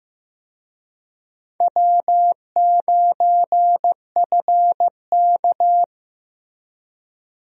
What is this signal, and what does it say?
Morse code, heard as a keyed tone, about 15 wpm, 700 Hz. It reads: W9FK
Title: USA Call Signs - 4 Characters - Repeated 2x - 15wpm